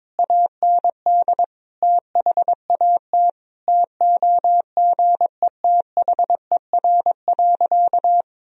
Morse code, heard as a keyed tone, about 22 wpm, 700 Hz. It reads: AND THAT TOGETHER.
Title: Sentences from Top 1000 Words - Review Part 1 - 22wpm